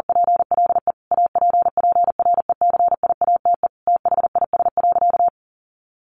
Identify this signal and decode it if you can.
Morse code, heard as a keyed tone, about 40 wpm, 700 Hz. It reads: PLE APPRECIATE THIS.
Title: Sentences from Top 1300 Words 40wpm